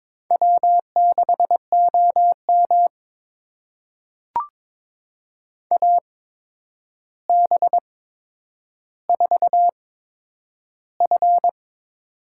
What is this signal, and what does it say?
Morse code, heard as a keyed tone, about 22 wpm, spaced as for 9 wpm, 700 Hz. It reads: W6OM AB4F
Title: USA Call Signs - 4 Characters - 8x Character Spacing - 22wpm